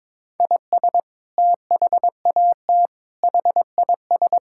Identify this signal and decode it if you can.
Morse code, heard as a keyed tone, about 22 wpm, 700 Hz. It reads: IS THAT HIS
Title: Sentences from Top 1000 Words - Review Part 5 - 22wpm